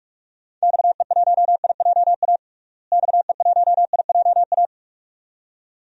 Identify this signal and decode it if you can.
Morse code, heard as a keyed tone, about 46 wpm, 700 Hz. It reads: XE1IJA XE1IJA
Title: DXCC-100 Call Signs - 6 Characters - Repeated 2x - 45wpm